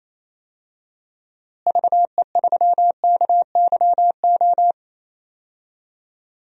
Morse code, heard as a keyed tone, about 27 wpm, 700 Hz. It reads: VE3KYO
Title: DXCC-100 Call Signs - 6 Characters - Repeated 2x - 28wpm